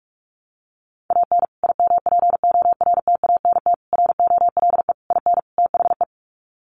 Morse code, heard as a keyed tone, about 44 wpm, 700 Hz. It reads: AN IMPORTANT ROLE IN THE
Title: Sets of 5 Words Encore2 45wpm